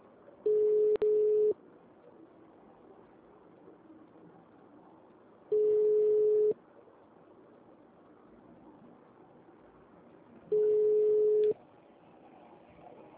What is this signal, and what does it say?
Ringback tone (Europe, 425 Hz).
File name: out- -2027- - - .459